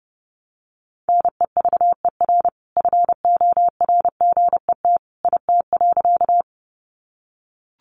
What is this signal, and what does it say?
Morse code, heard as a keyed tone, about 30 wpm, 700 Hz. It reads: NEVER FORGET IT.
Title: Sentences from Top 1200 Words 30wpm